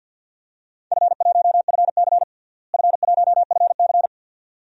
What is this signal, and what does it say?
Morse code, heard as a keyed tone, about 52 wpm, 700 Hz. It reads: F1FC F1FC
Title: DXCC-100 Call Signs - 4 Characters - Repeated 2x - 50wpm